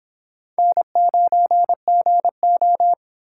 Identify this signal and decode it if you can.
Morse code, heard as a keyed tone, about 30 wpm, 700 Hz. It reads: N9GO